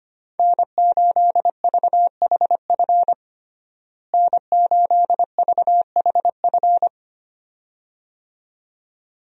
Morse code, heard as a keyed tone, about 25 wpm, 700 Hz. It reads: N8VHF N8VHF
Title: USA Call Signs - 5 Characters - Repeated 2x - 25wpm